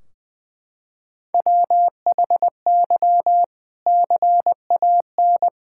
Morse code, heard as a keyed tone, about 20 wpm, 700 Hz. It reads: WHY CAN
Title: Sentences from Top 1600 Words 20wpm